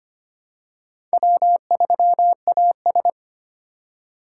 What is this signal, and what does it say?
Morse code, heard as a keyed tone, about 25 wpm, 700 Hz. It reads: W3AS
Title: USA Call Signs - 4 Characters - 2x Character Spacing - 25wpm